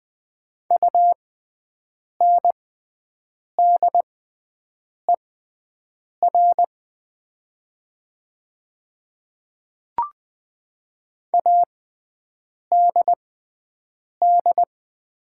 Morse code, heard as a keyed tone, about 20 wpm, spaced as for 7 wpm, 700 Hz. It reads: UNDER ADD
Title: 6x Character Spacing - Top 200 Words - 20wpm